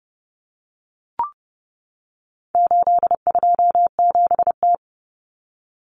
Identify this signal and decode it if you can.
Morse code, heard as a keyed tone, about 30 wpm, 700 Hz. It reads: 827T
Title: Common Cut Numbers - 4-Digit 30wpm